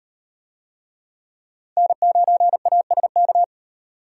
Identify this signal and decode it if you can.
Morse code, heard as a keyed tone, about 38 wpm, 700 Hz. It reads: N9ASK